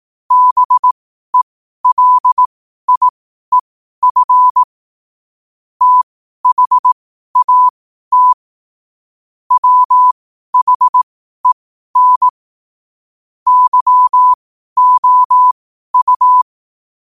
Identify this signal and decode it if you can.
Morse code, heard as a keyed tone, about 18 wpm, spaced as for 12 wpm, 1000 Hz. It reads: BELIEF THAT WHEN YOU
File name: qotdfiles / 12wpm